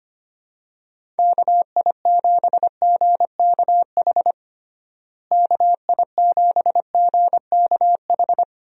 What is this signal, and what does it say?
Morse code, heard as a keyed tone, about 25 wpm, 700 Hz. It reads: KI7GKH KI7GKH
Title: USA Call Signs - 6 Characters - Repeated 2x - 25wpm